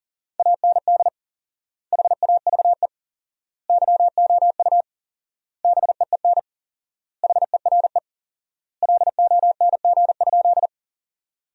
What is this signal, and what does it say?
Morse code, heard as a keyed tone, about 40 wpm, 700 Hz. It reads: AND HAVE YOU BEEN HERE LONG?
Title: 4x Word-Spacing - Sentences from Top 200 Words - 40wpm